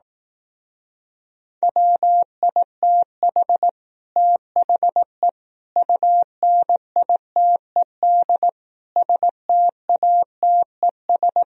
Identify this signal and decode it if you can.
Morse code, heard as a keyed tone, about 18 wpm, 700 Hz. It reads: WITH THE UNITED STATES
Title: Speed-Racing - Sets of 4 Words 15wpm